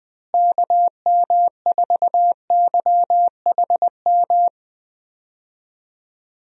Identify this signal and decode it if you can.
Morse code, heard as a keyed tone, about 20 wpm, 700 Hz. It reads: KM4YHM
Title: USA Call Signs - 6 Characters - Repeated 3x - 20wpm